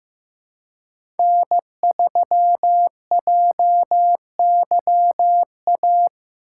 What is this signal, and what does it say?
Morse code, heard as a keyed tone, about 15 wpm, 700 Hz. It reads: N3JYA
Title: USA Call Signs - 5 Characters - 3x Character Spacing - 15wpm